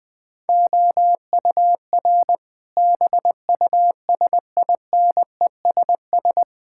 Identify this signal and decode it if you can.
Morse code, heard as a keyed tone, about 20 wpm, 700 Hz. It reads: OUR BUSINESS
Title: Sentences from Top 1000 Words - Review Part 6 - 20wpm